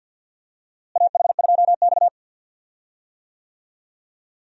Morse code, heard as a keyed tone, about 51 wpm, 700 Hz. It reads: AH2X
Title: USA Call Signs - 4 Characters - Repeated 2x - 50wpm